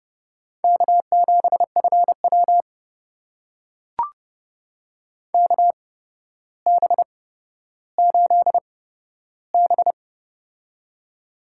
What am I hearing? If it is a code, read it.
Morse code, heard as a keyed tone, about 30 wpm, spaced as for 13 wpm, 700 Hz. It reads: K7FW KB8B
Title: USA Call Signs - 4 Characters - 8x Character Spacing - 30wpm